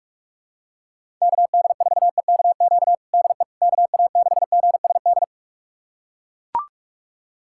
Morse code, heard as a keyed tone, about 41 wpm, 700 Hz. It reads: KD4EKQ DE KA6GSD
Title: USA Callsign DE Callsign - 6 Characters - 45wpm